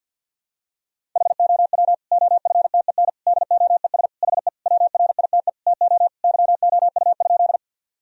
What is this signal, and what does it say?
Morse code, heard as a keyed tone, about 46 wpm, 700 Hz. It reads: HOW OFTEN DOES HE WRITE TO YOU?
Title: Sentences from Top 1000 Words - Review Part 1 - 50wpm